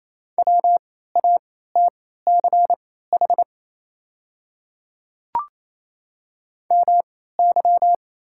Morse code, heard as a keyed tone, about 28 wpm, spaced as for 16 wpm, 700 Hz. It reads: WATCH MY
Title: 3x Character Spacing - Top 300 Words - 28wpm